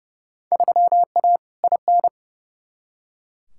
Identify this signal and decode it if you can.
Morse code, heard as a keyed tone, about 31 wpm, 700 Hz. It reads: 3A IN
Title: Field Day Exchanges - 30wpm